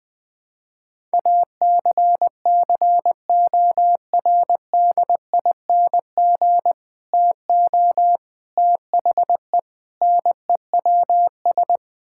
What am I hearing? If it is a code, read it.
Morse code, heard as a keyed tone, about 21 wpm, 700 Hz. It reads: ACCORDING TO THE NEWS
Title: Sentences with 3-Word Phrases - 20wpm